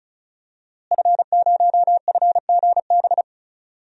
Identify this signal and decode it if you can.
Morse code, heard as a keyed tone, about 35 wpm, 700 Hz. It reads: F0FGB
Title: DXCC-100 Call Signs - 5 Characters - Repeated 3x - 35wpm